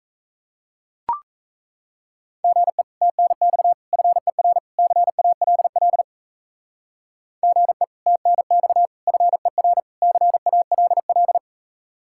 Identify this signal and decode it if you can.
Morse code, heard as a keyed tone, about 33 wpm, 700 Hz. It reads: GE TNX FER CALL GE TNX FER CALL
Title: Speed-Racing Common QSO Phrases - 28wpm